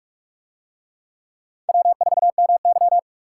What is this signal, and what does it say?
Morse code, heard as a keyed tone, about 42 wpm, 700 Hz. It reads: W4MY